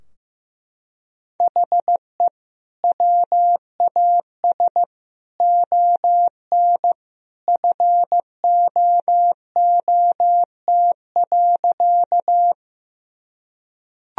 Morse code, heard as a keyed tone, about 15 wpm, 700 Hz. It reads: HE WAS ON FOOT.
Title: Sentences from Top 1000 Words - Review Part 3 - 15wpm